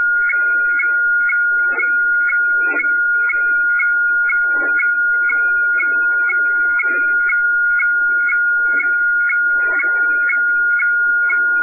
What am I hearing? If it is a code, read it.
Weather fax (radiofax).
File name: hfwefax / 12788.1.khz. .2025